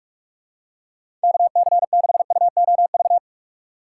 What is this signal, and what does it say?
Morse code, heard as a keyed tone, about 44 wpm, 700 Hz. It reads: KC6UOV
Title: USA Call Signs - 6 Characters - 6x Character Spacing - 45wpm